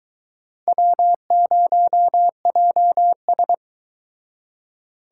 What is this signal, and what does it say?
Morse code, heard as a keyed tone, about 23 wpm, 700 Hz. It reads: W0JS